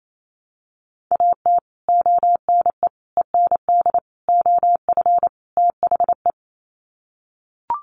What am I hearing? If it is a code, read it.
Morse code, heard as a keyed tone, about 28 wpm, 700 Hz. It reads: AT ONE END OF THE
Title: Sets of 5 Words Encore 28wpm